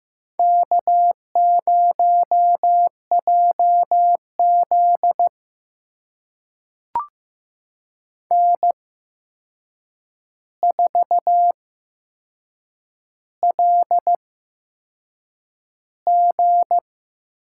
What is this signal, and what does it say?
Morse code, heard as a keyed tone, about 15 wpm, spaced as for 6 wpm, 700 Hz. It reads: K0JZ N4LG
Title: USA Call Signs - 4 Characters - 8x Character Spacing - 15wpm